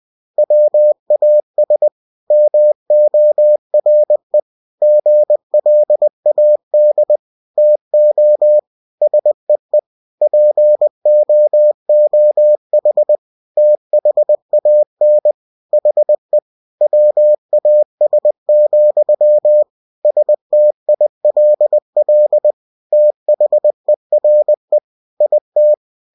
Morse code, heard as a keyed tone, about 20 wpm, 600 Hz. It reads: WAS MORE GLAD TO SEE POOH THAN HE WAS, STILL THERE IT
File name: cw-books / winnie-the-pooh / winnie-20wpm-page-0018